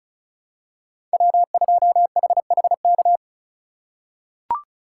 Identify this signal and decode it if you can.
Morse code, heard as a keyed tone, about 36 wpm, 700 Hz. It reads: W2HHK
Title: USA Call Signs - 5 Characters - 2x Character Spacing - 35wpm